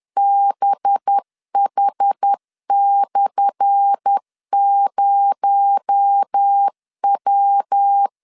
Morse code, heard as a keyed tone, about 10 wpm, 800 Hz. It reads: BH/0W
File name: LEZIONI / 10 SIGNS / 10A SLASH 10wpm 2808